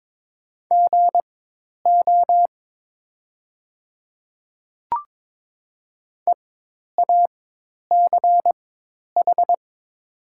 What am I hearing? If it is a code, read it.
Morse code, heard as a keyed tone, about 22 wpm, spaced as for 10 wpm, 700 Hz. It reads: GO EACH
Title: 4x Character Spacing - Top 100 Words - 22wpm